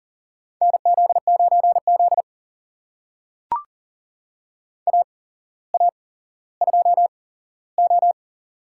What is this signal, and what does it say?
Morse code, heard as a keyed tone, about 40 wpm, spaced as for 17 wpm, 700 Hz. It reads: NZ9Z AA2O
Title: USA Call Signs - 8x Character Spacing - 40wpm